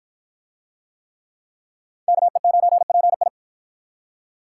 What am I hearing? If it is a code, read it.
Morse code, heard as a keyed tone, about 53 wpm, 700 Hz. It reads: KE9PI